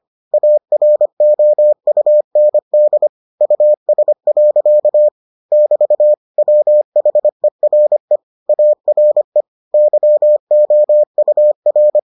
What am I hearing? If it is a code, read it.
Morse code, heard as a keyed tone, about 25 wpm, 600 Hz. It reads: AROUND US. <BT> WHERE ARE YOUR